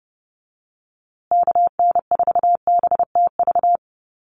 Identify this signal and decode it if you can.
Morse code, heard as a keyed tone, about 30 wpm, 700 Hz. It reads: KN4BTV
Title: USA Call Signs Repeated 2x 30wpm